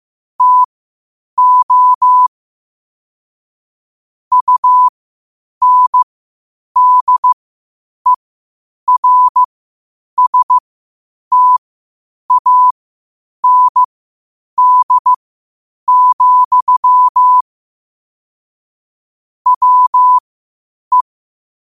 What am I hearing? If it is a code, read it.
Morse code, heard as a keyed tone, about 15 wpm, spaced as for 8 wpm, 1000 Hz. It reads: TO UNDERSTAND, WE
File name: qotdfiles / 8wpm